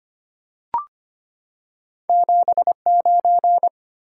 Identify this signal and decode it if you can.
Morse code, heard as a keyed tone, about 25 wpm, 700 Hz. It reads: 79
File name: Two Letter-Number Combinations - 25wpm